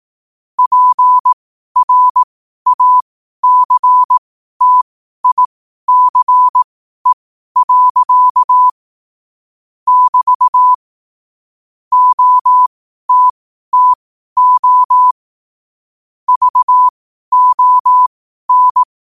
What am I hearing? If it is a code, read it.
Morse code, heard as a keyed tone, about 18 wpm, spaced as for 12 wpm, 1000 Hz. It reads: PRACTICE. <BT> OTTO VON
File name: qotdfiles / 12wpm